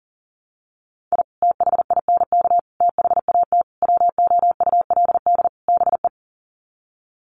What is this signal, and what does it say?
Morse code, heard as a keyed tone, about 40 wpm, 700 Hz. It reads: I THINK THAT WOULD BE
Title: Sets of 5 Words Encore2 40wpm